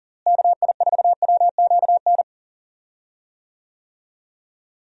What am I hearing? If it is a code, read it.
Morse code, heard as a keyed tone, about 40 wpm, 700 Hz. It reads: KI4WQN
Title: USA Call Signs - 6 Characters - Repeated 2x - 40wpm